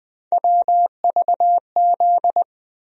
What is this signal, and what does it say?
Morse code, heard as a keyed tone, about 20 wpm, 700 Hz. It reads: WVZ
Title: USA Call Sign Postfixes - 3 Letters 20wpm